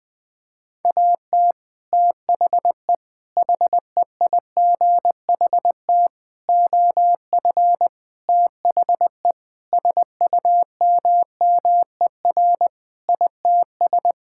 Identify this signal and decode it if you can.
Morse code, heard as a keyed tone, about 20 wpm, 700 Hz. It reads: AT THE HEIGHT OF THE SUMMER ITS
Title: Sentences with 5-Word Phrases - 20wpm